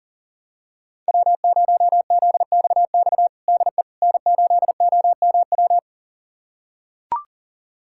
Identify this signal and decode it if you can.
Morse code, heard as a keyed tone, about 40 wpm, 700 Hz. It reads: W0ZXX DE N8OMW
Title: USA Callsign DE Callsign - 5 Characters - 40wpm